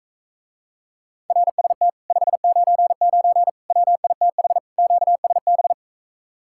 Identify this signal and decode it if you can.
Morse code, heard as a keyed tone, about 45 wpm, 700 Hz. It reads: RST 599 WITH QSB